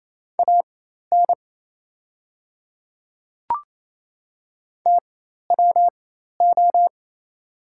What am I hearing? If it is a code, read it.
Morse code, heard as a keyed tone, about 28 wpm, spaced as for 13 wpm, 690 Hz. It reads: AN TWO
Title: 4x Character Spacing - Top 200 Words - 28wpm